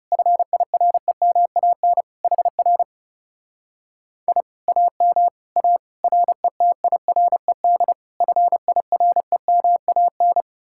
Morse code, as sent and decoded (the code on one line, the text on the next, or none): ..-. .. .-. -- .- -. / .... .-. / .. / .- -- / .- / .-. - .. .-. -.. / ..-. .. .-. -- .- -.
FIREMAN HR I AM A RETIRED FIREMAN